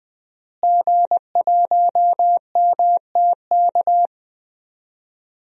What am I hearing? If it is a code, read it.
Morse code, heard as a keyed tone, about 20 wpm, 700 Hz. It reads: G1MTK
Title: DXCC-100 Call Signs - 5 Characters - Repeated 2x - 20wpm